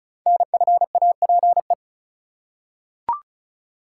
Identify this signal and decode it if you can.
Morse code, heard as a keyed tone, about 36 wpm, 700 Hz. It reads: NFAPE
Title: Code Groups - 5 Letters - 35wpm